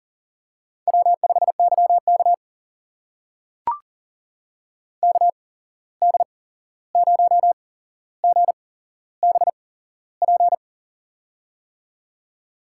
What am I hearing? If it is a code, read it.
Morse code, heard as a keyed tone, about 41 wpm, spaced as for 15 wpm, 700 Hz. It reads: W5YK KD0GBP
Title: USA Call Signs - 8x Character Spacing - 40wpm